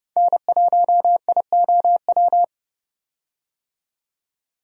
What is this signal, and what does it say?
Morse code, heard as a keyed tone, about 30 wpm, 700 Hz. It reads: N1IOW
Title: USA Call Signs - 5 Characters - 30wpm